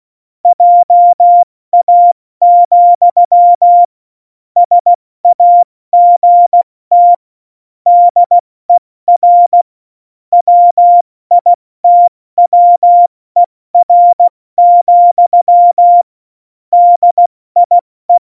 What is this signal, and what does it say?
Morse code, heard as a keyed tone, about 16 wpm, 700 Hz. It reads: JA, SAGT DER WITWER, DIE
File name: Morsekurs / Witze / Witz 21 Juli 2020, 70 F80